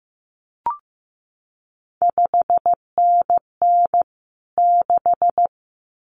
Morse code, heard as a keyed tone, about 15 wpm, 700 Hz. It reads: 5NN 6
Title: CQ WW DX Contest Exchange Encore - 15wpm